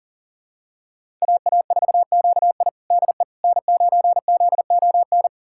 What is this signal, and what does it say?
Morse code, heard as a keyed tone, about 40 wpm, 700 Hz. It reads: AA4QI DE N9ZON
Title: USA Callsign DE Callsign - 5 Characters - 40wpm